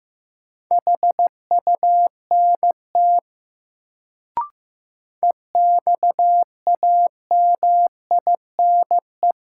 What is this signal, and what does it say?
Morse code, heard as a keyed tone, about 15 wpm, 700 Hz. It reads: HUNT EXAMINE
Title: Sentences from Top 1600 Words - New Word Drill Alt - 15wpm